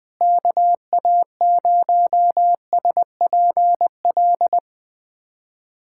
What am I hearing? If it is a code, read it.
Morse code, heard as a keyed tone, about 21 wpm, 700 Hz. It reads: KA0SPL
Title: USA Callsign DE Callsign - 6 Characters - 20wpm